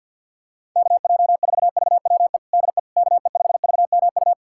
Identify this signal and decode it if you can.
Morse code, heard as a keyed tone, about 51 wpm, 700 Hz. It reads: KJ4VWE DE KE5VMU
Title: USA Callsign DE Callsign - 6 Characters - 50wpm